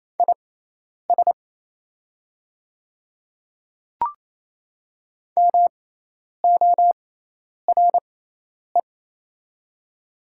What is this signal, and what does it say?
Morse code, heard as a keyed tone, about 28 wpm, spaced as for 10 wpm, 690 Hz. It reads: IS MORE
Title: 6x Character Spacing - Top 100 Words - 28wpm